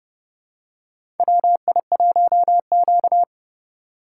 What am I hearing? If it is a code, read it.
Morse code, heard as a keyed tone, about 31 wpm, 700 Hz. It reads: WI1Q
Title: USA Call Signs - 4 Characters - 4x Character Spacing - 30wpm